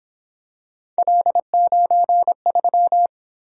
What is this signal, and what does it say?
Morse code, heard as a keyed tone, about 26 wpm, 700 Hz. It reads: L93
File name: Speed-Racing - Three Letter-Number Combinations - 22wpm